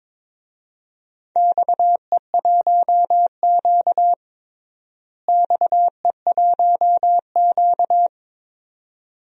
Morse code, heard as a keyed tone, about 21 wpm, 700 Hz. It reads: XE1Q XE1Q
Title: DXCC-100 Call Signs - 4 Characters - Repeated 2x - 22wpm